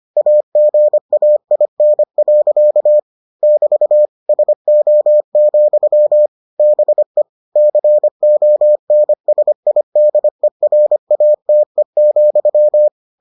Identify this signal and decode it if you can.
Morse code, heard as a keyed tone, about 25 wpm, 600 Hz. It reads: AGAIN. <BT> SO, BE CONSIDERATE,